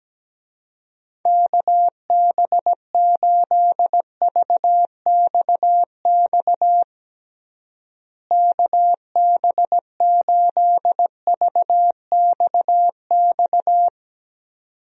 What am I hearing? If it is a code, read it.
Morse code, heard as a keyed tone, about 17 wpm, 700 Hz. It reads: KB8VXX KB8VXX
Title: USA Call Signs - 6 Characters - Repeated 2x - 17wpm